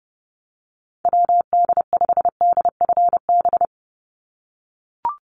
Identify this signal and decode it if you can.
Morse code, heard as a keyed tone, about 30 wpm, 700 Hz. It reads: WD5DFB
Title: USA Call Signs 30wpm